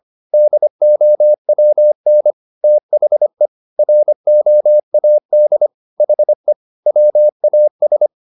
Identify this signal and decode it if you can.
Morse code, heard as a keyed tone, about 25 wpm, 600 Hz. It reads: DOWN THE ROAD HE WAS